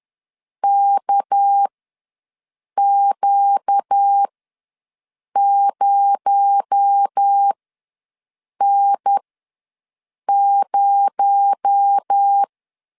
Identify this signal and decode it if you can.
Morse code, heard as a keyed tone, about 10 wpm, spaced as for 6 wpm, 800 Hz. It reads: KQ0N0
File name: LEZIONI / 6 PKX0 / 6 PKX0 ws 10wpm 4220